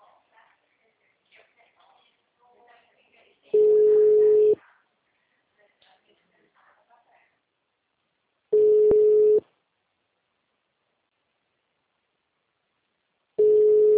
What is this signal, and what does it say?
Ringback tone (Europe, 425 Hz), with speech.